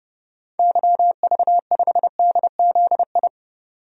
Morse code, heard as a keyed tone, about 31 wpm, 700 Hz. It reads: YV5DZI
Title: DXCC-100 Call Signs - 6 Characters - 30wpm